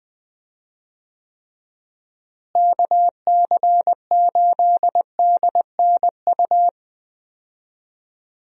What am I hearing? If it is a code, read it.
Morse code, heard as a keyed tone, about 20 wpm, 700 Hz. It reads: KC8DNU
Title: USA Call Signs - 6 Characters - Repeated 3x - 20wpm